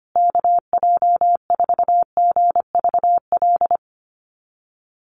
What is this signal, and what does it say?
Morse code, heard as a keyed tone, about 25 wpm, 700 Hz. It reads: KJ4GVL
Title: USA Call Signs Repeated 3x 25wpm